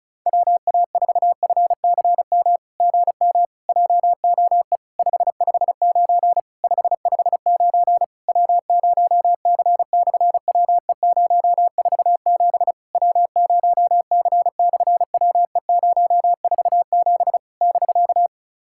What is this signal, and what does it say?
Morse code, heard as a keyed tone, about 33 wpm, 700 Hz. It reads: WA4FCM GM JOE 559 559 W0C/WE047 W0C/WE047 <BK>
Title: SOTA QSO Practice 35wpm